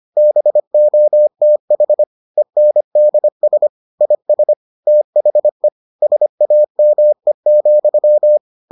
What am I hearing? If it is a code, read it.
Morse code, heard as a keyed tone, about 25 wpm, 600 Hz. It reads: BOTH ENDS IS THE SAME,